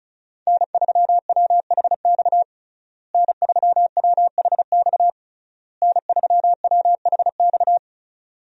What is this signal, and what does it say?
Morse code, heard as a keyed tone, about 36 wpm, 700 Hz. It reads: N3WHX N3WHX N3WHX
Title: USA Call Signs - 5 Characters - Repeated 3x - 35wpm